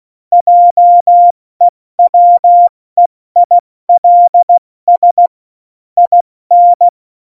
Morse code, heard as a keyed tone, about 16 wpm, 700 Hz. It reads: JEWEILS IN
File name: Morsekurs / Witze / Witz 13. Oktober 2020, 70 F80